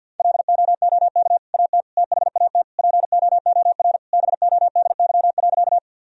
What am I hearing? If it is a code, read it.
Morse code, heard as a keyed tone, about 50 wpm, 700 Hz. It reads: LOOK AT THAT POOR BODY.
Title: Sentences from Top 1000 Words - Review Part 2 - 50wpm